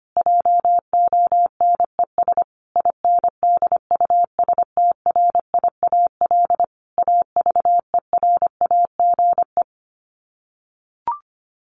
Morse code, heard as a keyed tone, about 25 wpm, 700 Hz. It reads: JONES INDUSTRIAL AVERAGE
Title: Sets of 5 Words 25wpm